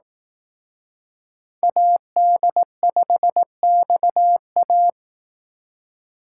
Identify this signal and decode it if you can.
Morse code, heard as a keyed tone, about 18 wpm, 700 Hz. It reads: AD5XA
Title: Speed Racing - USA 2x2 Call Signs 15wpm